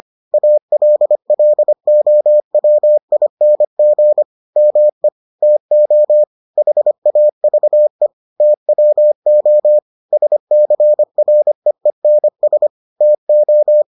Morse code, heard as a keyed tone, about 25 wpm, 600 Hz. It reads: ALLOWING ME TO HAVE TWO SCREENS TO